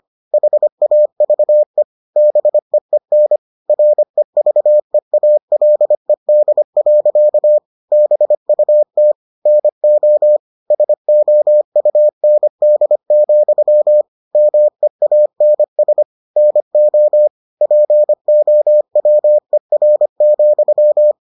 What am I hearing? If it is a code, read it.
Morse code, heard as a keyed tone, about 25 wpm, 600 Hz. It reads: HAVE BEEN REVEALED. BUT NO SOUND, MEANS NO POWER,